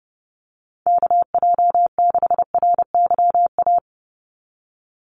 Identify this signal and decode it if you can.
Morse code, heard as a keyed tone, about 30 wpm, 700 Hz. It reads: KJ6RYA
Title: USA Call Signs 30wpm